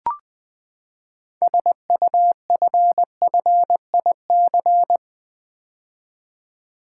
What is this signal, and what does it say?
Morse code, heard as a keyed tone, about 20 wpm, 700 Hz. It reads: SUFFIC
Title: Build A Word - 6 Letters At A Time - 12-Letter Words - 20wpm